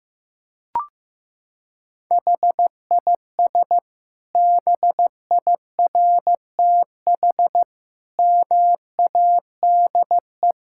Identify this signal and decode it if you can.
Morse code, heard as a keyed tone, about 15 wpm, 700 Hz. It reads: HIS BIRTH MADE
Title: Sentences from Top 1000 Words - Review Part 5 - 15wpm